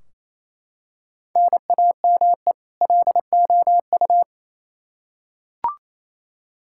Morse code, heard as a keyed tone, about 28 wpm, 700 Hz. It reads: NAME LOU
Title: Practice QSO - Exchange 1 - Beginning - 28wpm